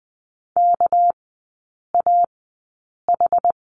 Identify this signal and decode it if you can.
Morse code, heard as a keyed tone, about 20 wpm, spaced as for 8 wpm, 700 Hz. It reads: KAH
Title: 198 Lesson 32e - Letter k - Three Characters 20wpm